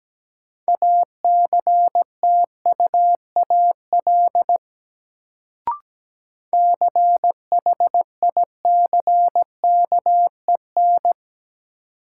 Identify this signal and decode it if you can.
Morse code, heard as a keyed tone, about 17 wpm, 700 Hz. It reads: ACTUAL CHICKEN
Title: Sentences from Top 1000 Words - Review Part 5 - New Word Drill - 17wpm